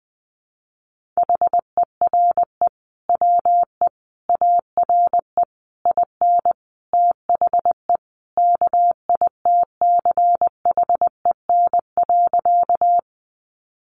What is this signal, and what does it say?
Morse code, heard as a keyed tone, about 20 wpm, 700 Hz. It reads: HERE WE ARE IN THE KITCHEN.
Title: Sentences from Top 1400 Words 20wpm